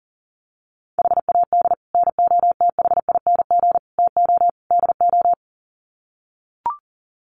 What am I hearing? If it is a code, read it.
Morse code, heard as a keyed tone, about 40 wpm, 700 Hz. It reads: HAD NOTHING TO DO
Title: Sets of 4 Words 40wpm